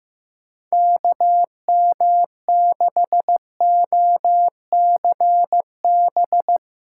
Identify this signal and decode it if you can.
Morse code, heard as a keyed tone, about 16 wpm, 700 Hz. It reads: KM6OCB
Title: USA Callsign DE Callsign - 6 Characters - 15wpm